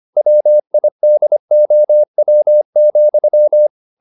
Morse code, heard as a keyed tone, about 25 wpm, 600 Hz. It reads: WIDOW,